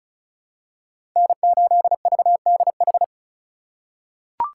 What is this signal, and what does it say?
Morse code, heard as a keyed tone, about 36 wpm, 700 Hz. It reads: N8VDH
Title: USA Call Signs - 5 Characters - 4x Character Spacing - 35wpm